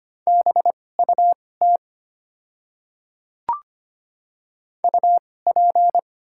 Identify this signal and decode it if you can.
Morse code, heard as a keyed tone, about 25 wpm, spaced as for 18 wpm, 690 Hz. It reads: BUT UP
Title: 2x Character Spacing - Top 100 Words - 25wpm